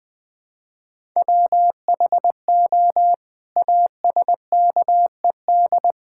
Morse code, heard as a keyed tone, about 20 wpm, 700 Hz. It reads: WHO ASKED
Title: Sets of 9 Words - 20wpm